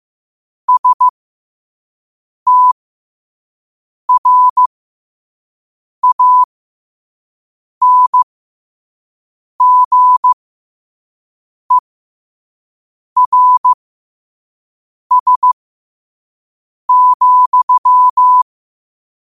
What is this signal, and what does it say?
Morse code, heard as a keyed tone, about 15 wpm, spaced as for 5 wpm, 1000 Hz. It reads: STRANGERS,